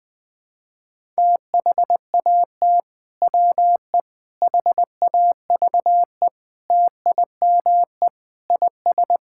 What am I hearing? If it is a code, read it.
Morse code, heard as a keyed tone, about 20 wpm, 700 Hz. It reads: THAT WE HAVE TIME IS
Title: Sentences with 3-Word Phrases - Spoken Beginning - 20wpm